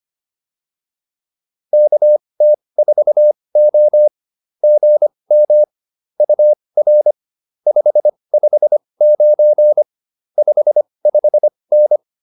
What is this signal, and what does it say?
Morse code, heard as a keyed tone, about 25 wpm, spaced as for 20 wpm, 590 Hz. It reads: KT4O GM UR 559 55N
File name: POTA207 20